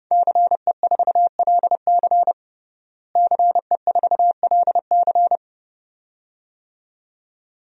Morse code, heard as a keyed tone, about 32 wpm, 700 Hz. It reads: CE4LC CE4LC
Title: DXCC-100 Call Signs - 5 Characters - Repeated 2x - 30wpm